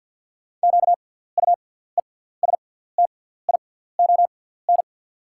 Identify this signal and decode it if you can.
Morse code, heard as a keyed tone, about 48 wpm, spaced as for 17 wpm, 700 Hz. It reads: QUESTION
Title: 6x Character Spacing - Top 500 Words - 50wpm